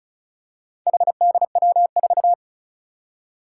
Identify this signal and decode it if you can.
Morse code, heard as a keyed tone, about 36 wpm, 700 Hz. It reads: HDW4
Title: Four Challenging Letter-Number Combinations 35wpm